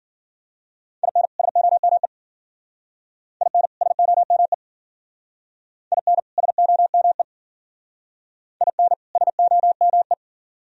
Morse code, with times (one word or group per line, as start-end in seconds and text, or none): 1.0-1.3 IN
1.4-2.1 SOME
3.4-3.7 IN
3.8-4.5 SOME
5.9-6.2 IN
6.4-7.2 SOME
8.6-8.9 IN
9.2-10.1 SOME